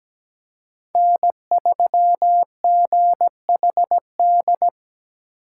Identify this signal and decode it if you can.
Morse code, heard as a keyed tone, about 17 wpm, 700 Hz. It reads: N3GHD
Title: USA Call Signs - 5 Characters - 4x Character Spacing - 17wpm